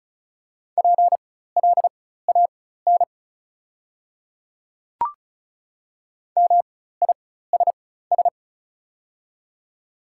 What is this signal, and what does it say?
Morse code, heard as a keyed tone, about 34 wpm, spaced as for 16 wpm, 690 Hz. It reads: PLAN MISS
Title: 4x Character Spacing - Top 500 Words - 35wpm